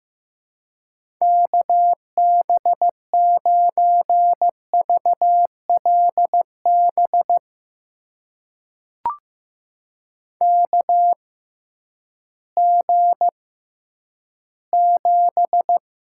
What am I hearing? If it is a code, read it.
Morse code, heard as a keyed tone, about 15 wpm, spaced as for 10 wpm, 700 Hz. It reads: KB9VLB KG7
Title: USA Call Signs - 6 Characters - 6x Character Spacing - 15wpm